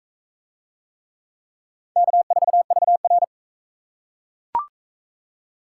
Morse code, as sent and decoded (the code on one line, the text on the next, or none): -.- ....- ...- .-.
K4VR